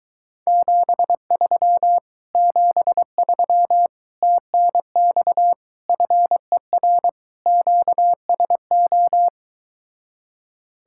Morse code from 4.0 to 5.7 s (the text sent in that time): TNX